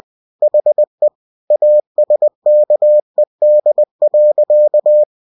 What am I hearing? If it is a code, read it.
Morse code, heard as a keyed tone, about 20 wpm, 600 Hz. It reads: HE ASKED.